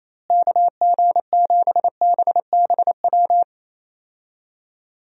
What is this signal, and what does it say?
Morse code, heard as a keyed tone, about 28 wpm, 700 Hz. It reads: KG7BBW
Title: USA Call Signs - 6 Characters - Repeated 3x - 28wpm